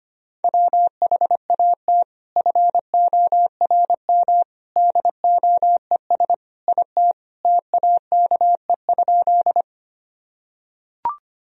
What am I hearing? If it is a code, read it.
Morse code, heard as a keyed tone, about 25 wpm, 700 Hz. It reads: WHAT FORM DOES IT TAKE?
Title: Sentences from Top 1000 Words - Review Part 1 - 25wpm